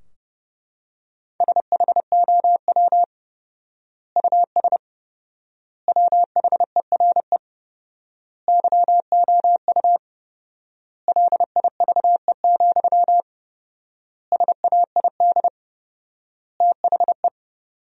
Morse code, heard as a keyed tone, about 30 wpm, 700 Hz. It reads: SHOW US WHERE YOU LIVE, SAID THE
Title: 4x Word-Spacing - Sentences from Top 200 Words - 30wpm